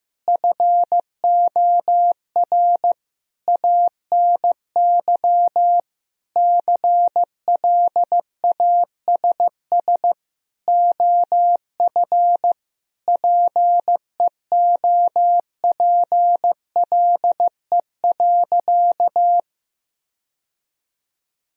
Morse code, heard as a keyed tone, about 15 wpm, 700 Hz. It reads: FOR ANY CLASS OF PEOPLE.
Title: Sentences from Top 1000 Words - Review Part 2 - 15wpm